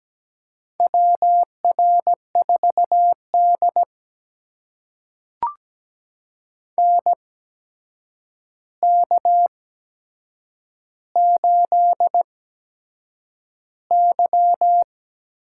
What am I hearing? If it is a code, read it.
Morse code, heard as a keyed tone, about 17 wpm, spaced as for 7 wpm, 700 Hz. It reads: WR4D NK8Y
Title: USA Call Signs - 4 Characters - 8x Character Spacing - 17wpm